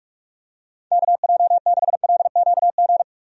Morse code, heard as a keyed tone, about 46 wpm, 700 Hz. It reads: KJ6LQG
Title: USA Call Signs - 6 Characters - 45wpm